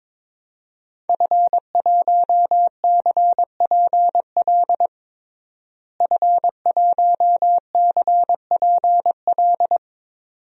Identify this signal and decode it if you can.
Morse code, heard as a keyed tone, about 22 wpm, 700 Hz. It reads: F1CPL F1CPL
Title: DXCC-100 Call Signs - 5 Characters - Repeated 2x - 22wpm